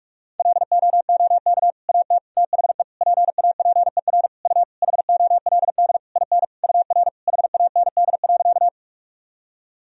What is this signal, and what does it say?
Morse code, heard as a keyed tone, about 46 wpm, 700 Hz. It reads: LOOK AT THE PAPER U HOLD IN UR HAND.
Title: Sentences from Top 400 Words and Abbreviations 45wpm